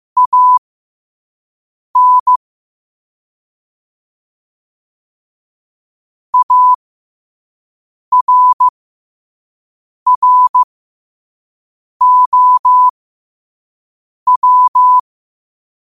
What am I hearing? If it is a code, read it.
Morse code, heard as a keyed tone, about 15 wpm, spaced as for 5 wpm, 1000 Hz. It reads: AN ARROW
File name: qotdfiles / 5wpm